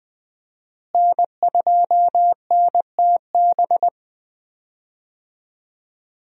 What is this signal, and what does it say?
Morse code, heard as a keyed tone, about 21 wpm, 700 Hz. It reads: N2NTB
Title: USA Call Signs - 5 Characters - Repeated 3x - 20wpm